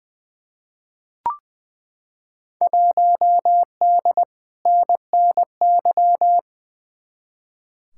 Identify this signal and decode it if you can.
Morse code, heard as a keyed tone, about 20 wpm, 700 Hz. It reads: 1D NNY
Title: Field Day Exchanges - 20wpm